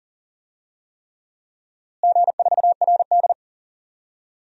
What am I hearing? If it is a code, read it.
Morse code, heard as a keyed tone, about 40 wpm, 700 Hz. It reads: G4RD
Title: DXCC-100 Call Signs - 4 Characters - Repeated 3x - 40wpm